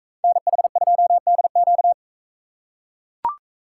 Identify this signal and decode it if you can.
Morse code, heard as a keyed tone, about 41 wpm, 700 Hz. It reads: NH2DQ